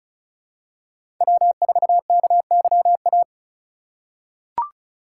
Morse code, heard as a keyed tone, about 36 wpm, 700 Hz. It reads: W4KYA
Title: USA Call Signs - 2x Character Spacing - 35wpm